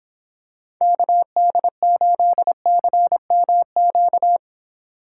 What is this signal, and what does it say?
Morse code, heard as a keyed tone, about 26 wpm, 700 Hz. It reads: KD8CMQ